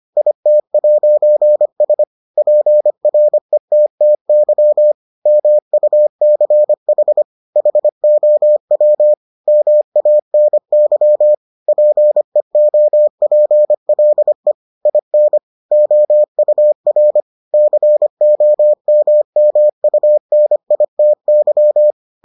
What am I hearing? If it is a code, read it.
Morse code, heard as a keyed tone, about 25 wpm, 600 Hz. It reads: IT'S PRETTY MUCH HOW MANY PEOPLE IN OUR COMMUNITY